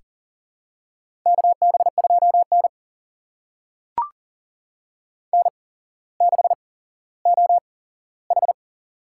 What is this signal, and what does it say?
Morse code, heard as a keyed tone, about 40 wpm, spaced as for 17 wpm, 700 Hz. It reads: KB2N N6OH
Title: USA Call Signs - 4 Characters - 8x Character Spacing - 40wpm